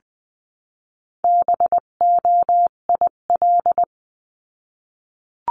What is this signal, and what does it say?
Morse code, heard as a keyed tone, about 20 wpm, 700 Hz. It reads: BOIL
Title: 258 Lesson 42c - Words 16-20wpm